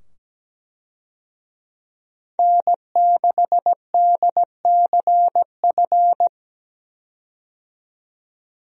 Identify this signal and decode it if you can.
Morse code, heard as a keyed tone, about 17 wpm, 700 Hz. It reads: N6DCF
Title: USA Call Signs - 5 Characters - 17wpm